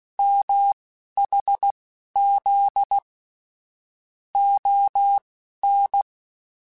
Morse code, heard as a keyed tone, about 16 wpm, spaced as for 11 wpm, 800 Hz. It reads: MHZ ON